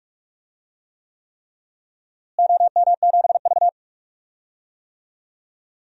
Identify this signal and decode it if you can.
Morse code, heard as a keyed tone, about 46 wpm, 700 Hz. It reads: OM7V
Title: DXCC-100 Call Signs - 4 Characters - Repeated 3x - 45wpm